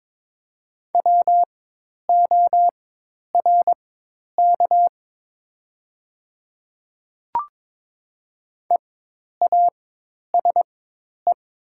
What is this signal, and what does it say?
Morse code, heard as a keyed tone, about 22 wpm, spaced as for 10 wpm, 700 Hz. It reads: WORK EASE
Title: 4x Character Spacing - Top 300 Words - 22wpm